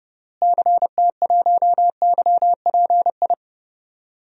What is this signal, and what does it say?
Morse code, heard as a keyed tone, about 30 wpm, 700 Hz. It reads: CT1YPI
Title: DXCC-100 Call Signs - 6 Characters - Repeated 3x - 30wpm